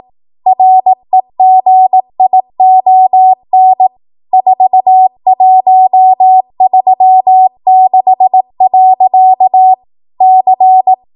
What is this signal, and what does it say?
Morse code, heard as a keyed tone, about 18 wpm, 750 Hz. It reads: REGION 4136. C
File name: Morse / 18WPM